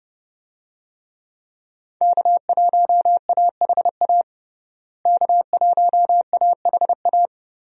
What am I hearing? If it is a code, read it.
Morse code, heard as a keyed tone, about 30 wpm, 700 Hz. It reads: K1AHA K1AHA
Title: USA Call Signs - 5 Characters - Repeated 3x - 30wpm